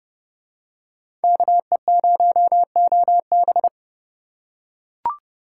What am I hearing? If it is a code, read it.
Morse code, heard as a keyed tone, about 26 wpm, 700 Hz. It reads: KE0OB